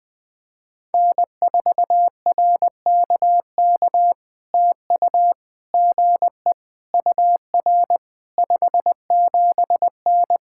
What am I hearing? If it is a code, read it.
Morse code, heard as a keyed tone, about 20 wpm, 700 Hz. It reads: N4RKK TU GE UR 57N
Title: POTA QSO Practice - 20wpm